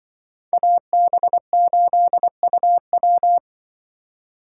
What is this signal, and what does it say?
Morse code, heard as a keyed tone, about 24 wpm, 700 Hz. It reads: AB8UW
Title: Speed Racing - USA 2x2 Call Signs 20wpm